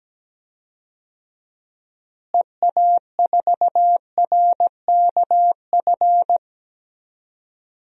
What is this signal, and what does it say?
Morse code, heard as a keyed tone, about 17 wpm, 700 Hz. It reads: EA4RKF
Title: DXCC-100 Call Signs - 6 Characters - Repeated 3x - 17wpm